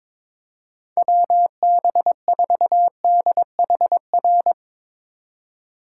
Morse code, heard as a keyed tone, about 24 wpm, 700 Hz. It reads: WB4DHR
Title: Speed Racing - USA 2x3 Call Signs 22wpm